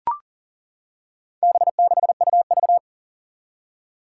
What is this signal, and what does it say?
Morse code, heard as a keyed tone, about 40 wpm, 700 Hz. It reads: B6UV
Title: Four Challenging Letter-Number Combinations 40wpm